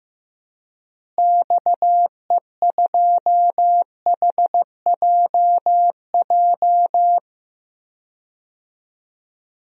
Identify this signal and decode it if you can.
Morse code, heard as a keyed tone, about 15 wpm, 700 Hz. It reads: XE2HJJ
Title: DXCC-100 Call Signs - 6 Characters - Repeated 3x - 15wpm